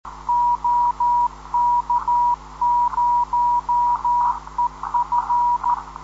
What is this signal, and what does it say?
Morse code, heard as a keyed tone, about 13 wpm, 960 Hz. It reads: OK0EF
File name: 10.133kHz